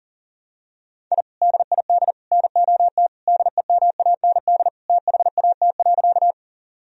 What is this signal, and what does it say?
Morse code, heard as a keyed tone, about 40 wpm, 700 Hz. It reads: I DID NOT DEMAND THAT.
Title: Sentences from Top 1000 Words - Review Part 4 - 40wpm